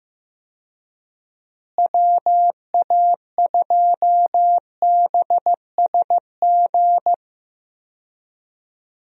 Morse code, heard as a keyed tone, about 15 wpm, 700 Hz. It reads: WA2BSG
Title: USA Call Signs - 6 Characters - Repeated 2x - 15wpm